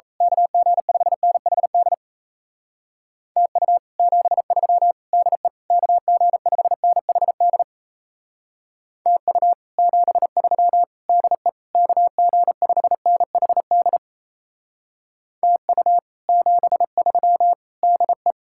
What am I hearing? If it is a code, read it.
Morse code, heard as a keyed tone, about 36 wpm, 700 Hz. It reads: KG5NHD TU 73 DE KG5NHD TU 73 DE KG5NHD TU 73 DE